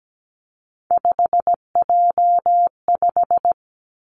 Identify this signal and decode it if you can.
Morse code, heard as a keyed tone, about 17 wpm, 700 Hz. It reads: 5J5
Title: Three Challenging Letter-Number Combinations 17wpm